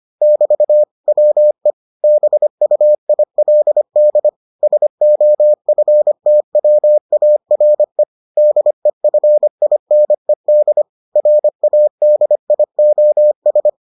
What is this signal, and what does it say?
Morse code, heard as a keyed tone, about 25 wpm, 600 Hz. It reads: <BT> WE BUILD SOFTWARE DEFINED RADIOS